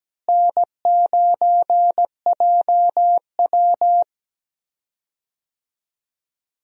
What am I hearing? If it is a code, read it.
Morse code, heard as a keyed tone, about 17 wpm, 700 Hz. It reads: N9JW
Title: USA Call Signs - 4 Characters - Repeated 3x - 17wpm